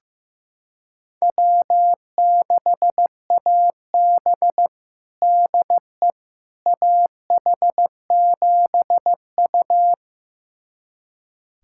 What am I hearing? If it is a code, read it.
Morse code, heard as a keyed tone, about 15 wpm, 700 Hz. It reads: W6AB DE AH7U
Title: USA Callsign DE Callsign - 4 Characters - 15wpm